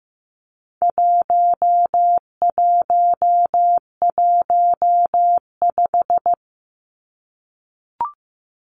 Morse code, heard as a keyed tone, about 15 wpm, 700 Hz. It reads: 1115
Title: Numbers - 4-Digit 15wpm